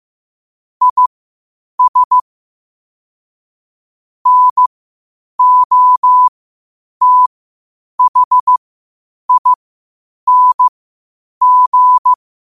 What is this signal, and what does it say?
Morse code, heard as a keyed tone, about 15 wpm, spaced as for 8 wpm, 1000 Hz. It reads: IS NOTHING
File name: qotdfiles / 8wpm